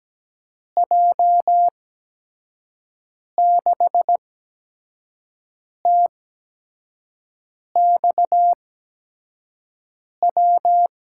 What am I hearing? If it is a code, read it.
Morse code, heard as a keyed tone, about 17 wpm, spaced as for 5 wpm, 700 Hz. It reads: J6TXW
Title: USA Call Signs - 6 Characters - 8x Character Spacing - 17wpm